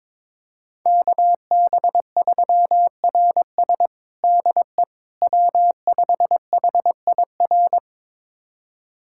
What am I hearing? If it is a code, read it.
Morse code, heard as a keyed tone, about 22 wpm, 700 Hz. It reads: KB3RS DE W5HIR
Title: USA Callsign DE Callsign 22wpm